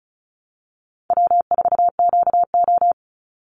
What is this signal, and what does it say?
Morse code, heard as a keyed tone, about 34 wpm, 700 Hz. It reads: W4QO
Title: Easy Call Signs Repeated 2x - 35wpm